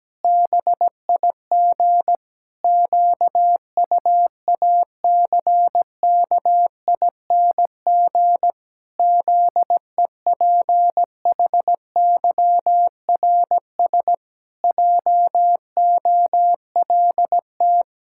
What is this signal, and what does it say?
Morse code, heard as a keyed tone, about 17 wpm, 700 Hz. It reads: BIG QUACKING ZEPHYRS JOLT
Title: Pangram Challenge v2 - 17wpm